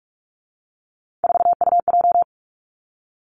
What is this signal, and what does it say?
Morse code, heard as a keyed tone, about 46 wpm, 700 Hz. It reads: 4UJ
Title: Three Challenging Letter-Number Combinations 45wpm